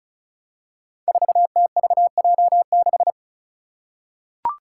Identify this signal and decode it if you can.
Morse code, heard as a keyed tone, about 36 wpm, 700 Hz. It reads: 4TVJ6
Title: Five Challenging Letter-Number Combinations 35wpm